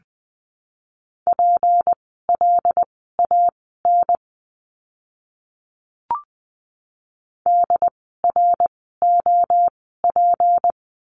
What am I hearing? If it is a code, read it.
Morse code, heard as a keyed tone, about 20 wpm, spaced as for 15 wpm, 700 Hz. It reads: PLAN DROP
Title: 153 Lesson 25f - Letter p - Words 20wpm